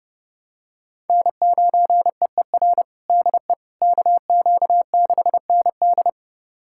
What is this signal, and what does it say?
Morse code, heard as a keyed tone, about 30 wpm, 700 Hz. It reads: N9EER DE KQ6ND
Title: USA Callsign DE Callsign - 5 Characters - 30wpm